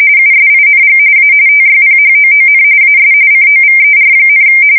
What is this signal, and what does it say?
RTTY, about 45.45 baud. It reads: USING A TYPHOON AS COVER,
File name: fsk170 / battleshipnj